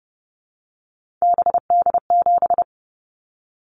Morse code, heard as a keyed tone, about 30 wpm, 700 Hz. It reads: BD7
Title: Callsign Prefixes for Top 100 DXCC - 30wpm